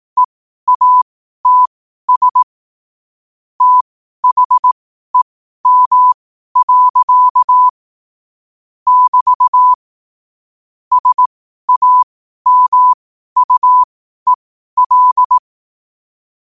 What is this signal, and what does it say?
Morse code, heard as a keyed tone, about 18 wpm, spaced as for 12 wpm, 1000 Hz. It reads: EATS THEM. <BT> SAMUEL
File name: qotdfiles / 12wpm